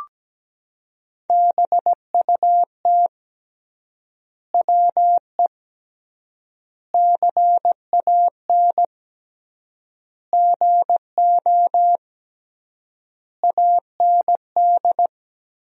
Morse code, heard as a keyed tone, about 17 wpm, 700 Hz. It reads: BUT WE CAN GO AND
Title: 3x Word-Spacing - Sentences from Top 100 Words - 17wpm